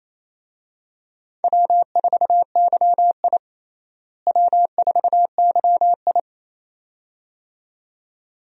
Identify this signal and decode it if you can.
Morse code, heard as a keyed tone, about 28 wpm, 700 Hz. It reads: W4YI W4YI
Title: USA Call Signs - 4 Characters - Repeated 2x - 28wpm